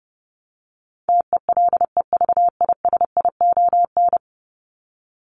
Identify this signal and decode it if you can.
Morse code, heard as a keyed tone, about 30 wpm, 700 Hz. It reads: TELEVISION
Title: Top 501-750 Words 30wpm